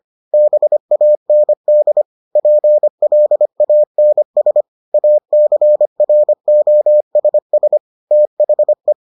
Morse code, heard as a keyed tone, about 25 wpm, 600 Hz. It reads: BAND PLANS ACROSS THE